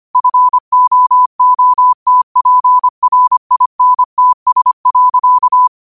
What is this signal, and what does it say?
Morse code, heard as a keyed tone, about 25 wpm, 1000 Hz. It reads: FOOTPRINTS.